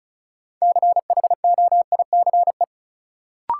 Morse code, heard as a keyed tone, about 36 wpm, 700 Hz. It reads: CHOICE
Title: Sentences from Top 900 Words - New Word Drill - 35wpm